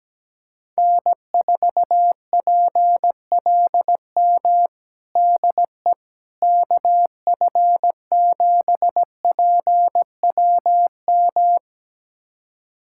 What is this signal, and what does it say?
Morse code, heard as a keyed tone, about 17 wpm, 700 Hz. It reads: N4PLM DE KF7PWM
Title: USA Callsign DE Callsign 17wpm